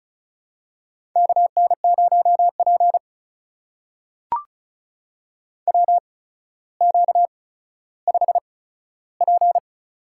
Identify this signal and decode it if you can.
Morse code, heard as a keyed tone, about 36 wpm, spaced as for 15 wpm, 700 Hz. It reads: KN0P WQ5P
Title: USA Call Signs - 4 Characters - 8x Character Spacing - 35wpm